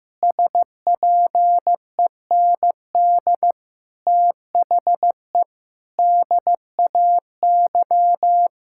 Morse code, heard as a keyed tone, about 16 wpm, 700 Hz. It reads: SPEND THE DAY
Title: Sentences from Top 1000 Words - Review Part 5 - 15wpm